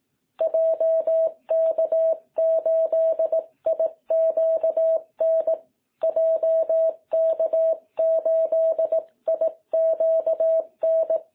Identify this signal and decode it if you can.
Morse code, heard as a keyed tone, about 18 wpm, 660 Hz. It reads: JK8IQN JK8IQN
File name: jh8cbh / 2021cw